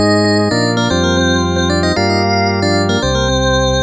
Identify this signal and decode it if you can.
There is no code.